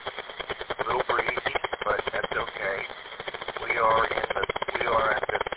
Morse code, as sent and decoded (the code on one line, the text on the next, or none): none